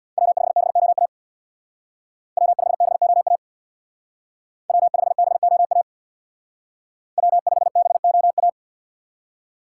Morse code, heard as a keyed tone, about 61 wpm, 700 Hz. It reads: W5BOA W5BOA W5BOA W5BOA
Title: Speed Racing - USA 1x3 Call Signs 50wpm